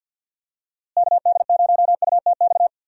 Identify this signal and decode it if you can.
Morse code, heard as a keyed tone, about 48 wpm, 700 Hz. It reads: KD0UTX
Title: USA Call Signs - 6 Characters - Repeated 2x - 50wpm